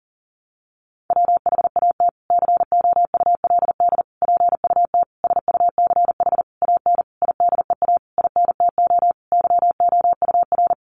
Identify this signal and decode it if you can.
Morse code, heard as a keyed tone, about 40 wpm, 700 Hz. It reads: WHAT COULD PUT SUCH AN IDEA INTO YOUR
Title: Sentences from Top 600 Words 40wpm